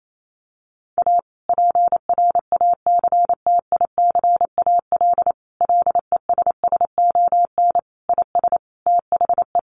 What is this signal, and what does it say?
Morse code, heard as a keyed tone, about 28 wpm, 700 Hz. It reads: A PRACTICAL LESSON IS THE
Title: Sentences from Top 1400 Words 28wpm